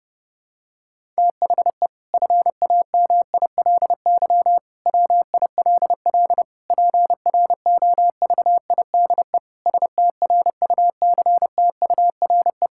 Morse code, heard as a keyed tone, about 30 wpm, 700 Hz. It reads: THE FAMILY WILL PROVIDE STRUCTURE
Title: Sentences from Top 1000 Words - Review Part 6 - 30wpm